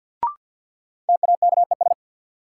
Morse code, heard as a keyed tone, about 53 wpm, 700 Hz. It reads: TAKES
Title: Sets of 5 Words Encore - New Word Drill - 50wpm